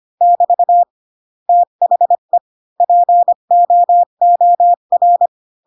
Morse code, heard as a keyed tone, about 25 wpm, 700 Hz. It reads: <BT> THE POOR
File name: Pinocchio / Pinocchio ch7 20 25 ik1hge